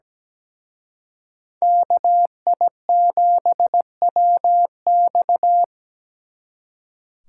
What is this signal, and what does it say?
Morse code, heard as a keyed tone, about 20 wpm, 700 Hz. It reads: KI7WX
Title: Speed Racing - USA 2x2 Call Signs 17wpm